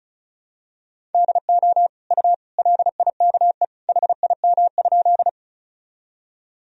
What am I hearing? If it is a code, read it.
Morse code, heard as a keyed tone, about 34 wpm, 700 Hz. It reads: DO U LIKE HIM?
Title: Sentences from Top 100 Words and Abbreviations 35wpm